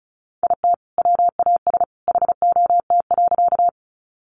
Morse code, heard as a keyed tone, about 34 wpm, 700 Hz. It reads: IT WAS HOT.
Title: Sentences from Top 500 Words 35wpm